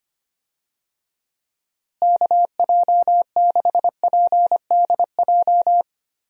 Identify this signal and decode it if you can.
Morse code, heard as a keyed tone, about 25 wpm, 700 Hz. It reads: KJ6PDJ
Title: USA Call Signs - 6 Characters - Repeated 2x - 25wpm